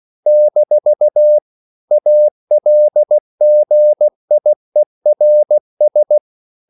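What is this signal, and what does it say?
Morse code, heard as a keyed tone, about 16 wpm, 600 Hz. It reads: - ALGIERS
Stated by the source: CW practice of national capitals at 16 words per minute, from ON6ZQ.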